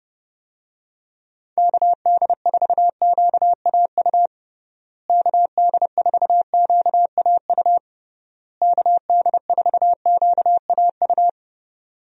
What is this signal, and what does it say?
Morse code, heard as a keyed tone, about 30 wpm, 700 Hz. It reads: KD4QAU KD4QAU KD4QAU
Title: USA Call Signs - 6 Characters - Repeated 3x - 30wpm